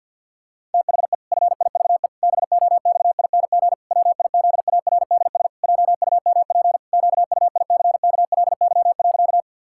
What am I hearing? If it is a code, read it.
Morse code, heard as a keyed tone, about 50 wpm, 700 Hz. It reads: THE FIVE BOXING WIZARDS JUMP QUICKLY.
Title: Pangram Challenge v2 - 50wpm